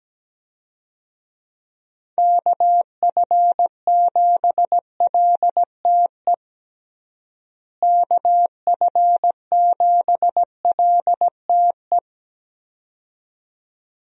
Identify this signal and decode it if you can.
Morse code, heard as a keyed tone, about 17 wpm, 700 Hz. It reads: KF7LTE KF7LTE
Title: USA Call Signs - 6 Characters - Repeated 2x - 17wpm